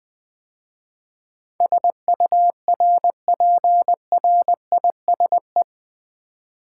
Morse code, heard as a keyed tone, about 20 wpm, 700 Hz. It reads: SURPRISE
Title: 8 Letter Words - First 3 Letters Given - 20wpm